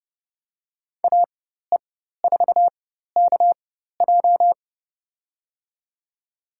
Morse code, heard as a keyed tone, about 30 wpm, spaced as for 14 wpm, 700 Hz. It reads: AE4KJ
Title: USA Call Signs - 5 Characters - 4x Character Spacing - 30wpm